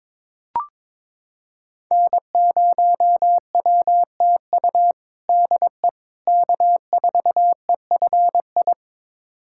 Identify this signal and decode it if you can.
Morse code, heard as a keyed tone, about 22 wpm, 700 Hz. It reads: N0WTU DE K4EFI
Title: USA Callsign DE Callsign - 5 Characters - 22wpm